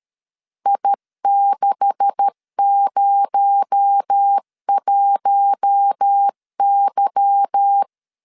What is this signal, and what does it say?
Morse code, heard as a keyed tone, about 13 wpm, 800 Hz. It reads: I601Y
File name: LEZIONI / 10 SIGNS / 10B INT 12wpm 2414